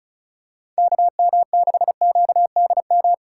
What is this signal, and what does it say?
Morse code, heard as a keyed tone, about 34 wpm, 700 Hz. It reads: KM6QDM
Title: USA Call Signs - 4x Character Spacing - 35wpm